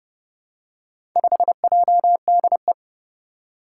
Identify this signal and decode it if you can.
Morse code, heard as a keyed tone, about 30 wpm, 700 Hz. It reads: 5JDE
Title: Four Challenging Letter-Number Combinations 30wpm